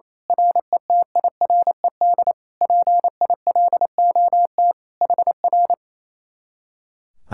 Morse code, heard as a keyed tone, about 27 wpm, 700 Hz. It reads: RETIRED PILOT HR
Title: Speed-Racing Common QSO Phrases Encore - 28wpm